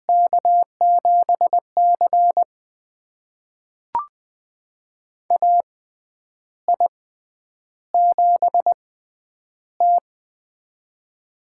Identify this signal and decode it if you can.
Morse code, heard as a keyed tone, about 20 wpm, spaced as for 9 wpm, 700 Hz. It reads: K7C AI7T
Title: USA Call Signs - 4 Characters - 6x Character Spacing - 20wpm